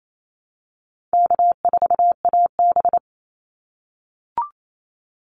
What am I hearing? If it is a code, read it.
Morse code, heard as a keyed tone, about 27 wpm, 700 Hz. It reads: K4AB
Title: Easy Call Signs - 28wpm